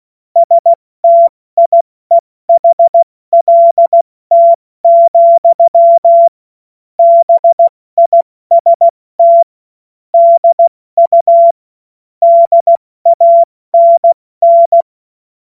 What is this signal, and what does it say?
Morse code, heard as a keyed tone, about 16 wpm, 680 Hz. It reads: STIEHLT, BIST DU DANN
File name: Morsekurs / Witze / Witz 04. Oktober 2022 70 F80